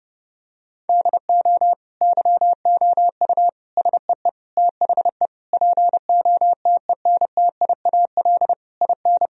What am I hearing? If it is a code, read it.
Morse code, heard as a keyed tone, about 32 wpm, 700 Hz. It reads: DO YOU SEE THE POTENTIAL IN
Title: Sentences from Top 1000 Words - Review Part 4 - 30wpm